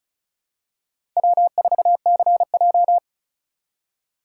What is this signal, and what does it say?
Morse code, heard as a keyed tone, about 36 wpm, 700 Hz. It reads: W4CJ
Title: USA Call Signs - 3x Character Spacing - 35wpm